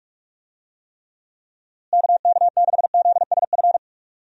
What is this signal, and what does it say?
Morse code, heard as a keyed tone, about 42 wpm, 700 Hz. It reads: KK6ZSF
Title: USA Call Signs - 6 Characters - Repeated 2x - 45wpm